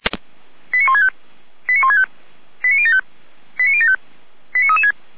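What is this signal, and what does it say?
Five-tone selective call (ZVEI), heard as quick runs of five tones, tones of about 70 ms. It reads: 89165 89165 89075 89075 89336
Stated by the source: ZVEI five-tone call received by SDR.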